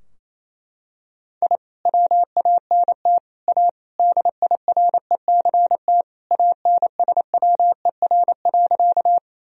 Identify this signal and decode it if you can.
Morse code, heard as a keyed tone, about 28 wpm, 700 Hz. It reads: I WANT A DIRECT ANSWER.
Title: Sentences from Top 1000 Words - Review Part 2 - 28wpm